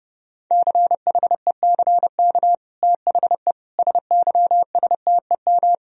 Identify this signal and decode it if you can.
Morse code, heard as a keyed tone, about 30 wpm, 700 Hz. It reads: CHECK THE SYSTEM
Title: Sentences from Top 1000 Words - Review Part 3 - 30wpm